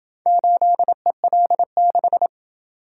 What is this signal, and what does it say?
Morse code, heard as a keyed tone, about 27 wpm, 700 Hz. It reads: 8EL6